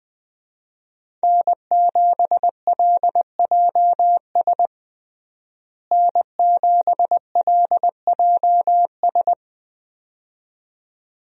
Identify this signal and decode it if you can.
Morse code, heard as a keyed tone, about 21 wpm, 700 Hz. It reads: N7LJS N7LJS
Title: USA Call Signs - 5 Characters - Repeated 2x - 20wpm